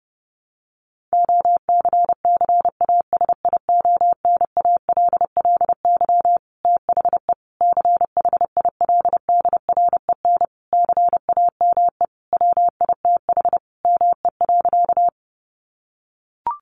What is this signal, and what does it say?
Morse code, heard as a keyed tone, about 30 wpm, 700 Hz. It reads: OCCASIONALLY THE CHILDREN CAME WITH ME.
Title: Sentences from Top 1300 Words 30wpm